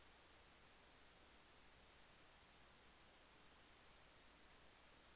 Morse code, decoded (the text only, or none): none